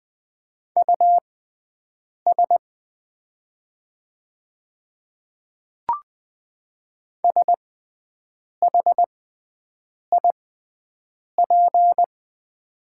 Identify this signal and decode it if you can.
Morse code, heard as a keyed tone, about 20 wpm, spaced as for 7 wpm, 700 Hz. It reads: US SHIP
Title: 6x Character Spacing - Top 500 Words - 20wpm